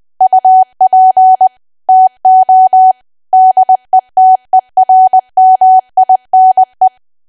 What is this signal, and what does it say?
Morse code, heard as a keyed tone, about 20 wpm, 750 Hz. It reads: UP TO DETERMINE
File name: Archive / 20 WPM / 20WPM